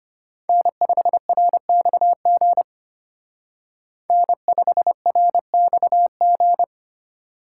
Morse code, heard as a keyed tone, about 29 wpm, 700 Hz. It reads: N5RXG N5RXG
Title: Speed Racing - USA 1x3 Call Signs 25wpm